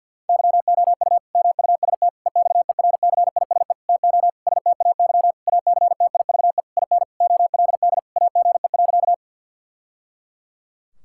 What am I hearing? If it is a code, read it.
Morse code, heard as a keyed tone, about 50 wpm, 700 Hz. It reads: YOU MUST EXERCISE TO STAY ACTIVE IN OLD AGE.
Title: Sentences from Top 1000 Words - Review Part 6 - 50wpm